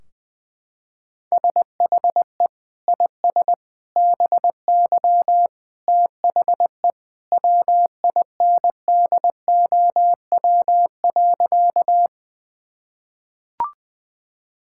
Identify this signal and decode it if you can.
Morse code, heard as a keyed tone, about 21 wpm, 700 Hz. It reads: SHE IS BY THE WINDOW.
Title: Sentences from Top 1600 Words 20wpm